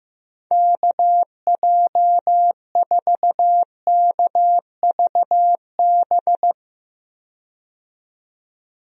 Morse code, heard as a keyed tone, about 15 wpm, 700 Hz. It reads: KJ4KVB
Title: USA Call Signs - 6 Characters - 15wpm